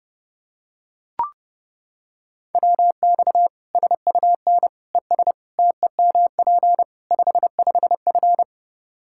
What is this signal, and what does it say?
Morse code, heard as a keyed tone, about 30 wpm, 700 Hz. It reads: WX SUN ES TEMP 55F
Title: Practice QSO - Exchange 2 - Info - 30wpm